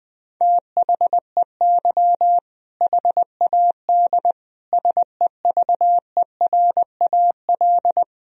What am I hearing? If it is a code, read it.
Morse code, heard as a keyed tone, about 20 wpm, 700 Hz. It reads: THEY HAD SEVERAL
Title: Sentences from Top 1000 Words - Review Part 2 - 20wpm